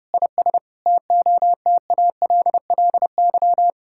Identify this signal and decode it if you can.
Morse code, heard as a keyed tone, about 30 wpm, 700 Hz. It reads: IS TOTALLY
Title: Sentences from Top 1600 Words 30wpm